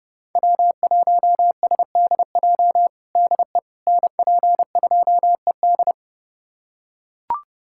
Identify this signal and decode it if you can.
Morse code, heard as a keyed tone, about 32 wpm, 700 Hz. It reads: W1SDJ DE NP2ED
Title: USA Callsign DE Callsign - 5 Characters - 30wpm